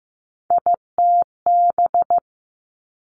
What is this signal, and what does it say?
Morse code, heard as a keyed tone, about 15 wpm, 700 Hz. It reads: ITB
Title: Three Challenging Letter-Number Combinations 15wpm